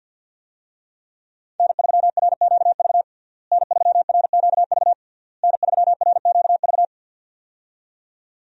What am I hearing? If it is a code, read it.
Morse code, heard as a keyed tone, about 50 wpm, 700 Hz. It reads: N3RQV N3RQV N3RQV
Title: USA Call Signs - 5 Characters - Repeated 3x - 50wpm